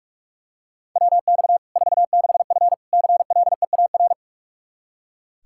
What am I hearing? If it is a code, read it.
Morse code, heard as a keyed tone, about 46 wpm, 700 Hz. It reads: WX 46F CLEAR
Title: Practice QSO - Exchange 2 - Info - 45wpm